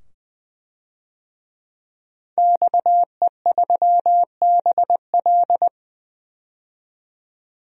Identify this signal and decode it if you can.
Morse code, heard as a keyed tone, about 20 wpm, 700 Hz. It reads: XE3BL
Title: DXCC-100 Call Signs - 5 Characters - 20wpm